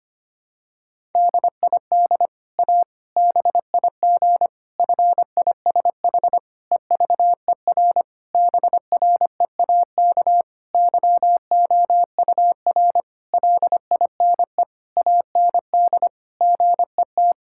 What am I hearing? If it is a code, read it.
Morse code, heard as a keyed tone, about 25 wpm, 700 Hz. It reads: DID A BIG FISH EVER BREAK YOUR LINE AND GET
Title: Sentences from Top 1000 Words - Review Part 6 - 25wpm